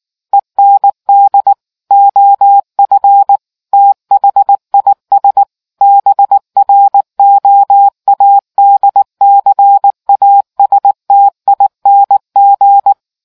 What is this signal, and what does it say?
Morse code, heard as a keyed tone, about 19 wpm, 800 Hz. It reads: END OF THIS BROADCASTING